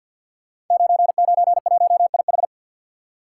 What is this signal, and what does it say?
Morse code, heard as a keyed tone, about 47 wpm, 700 Hz. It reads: 991IH
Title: Code Groups - 5 Letter:Numbers - 50wpm